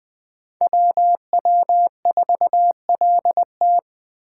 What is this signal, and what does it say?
Morse code, heard as a keyed tone, about 20 wpm, 700 Hz. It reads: WW4LT
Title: Speed Racing - USA 2x2 Call Signs 17wpm